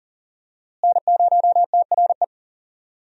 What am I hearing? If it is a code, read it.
Morse code, heard as a keyed tone, about 40 wpm, 700 Hz. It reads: N0TRE
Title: USA Call Signs - 5 Characters - Repeated 2x - 40wpm